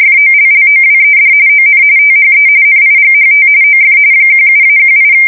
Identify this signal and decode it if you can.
RTTY, about 45.45 baud. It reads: FOLLOWED ON THE NIGHT OF 17-18
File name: fsk170 / USS-Missouri